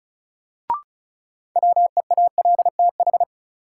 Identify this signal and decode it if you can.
Morse code, heard as a keyed tone, about 36 wpm, 700 Hz. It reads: WEALTH
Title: Sentences from Top 1000 Words - Review Part 4 - New Word Drill - 35wpm